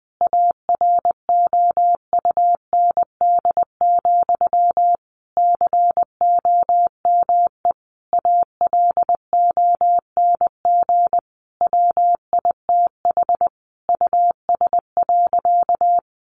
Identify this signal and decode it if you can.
Morse code, heard as a keyed tone, about 20 wpm, 700 Hz. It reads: AROUND, COME ALONG WITH US.
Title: Sentences from Top 1200 Words 20wpm